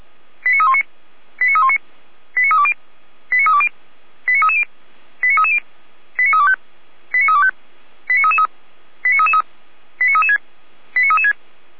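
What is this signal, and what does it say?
Five-tone selective call (ZVEI), heard as quick runs of five tones, tones of about 70 ms. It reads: 89319 89319 89320 89320 89330 89330 89325 89325 89333 89333 89336 89336
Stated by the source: ZVEI five-tone call received on SDR.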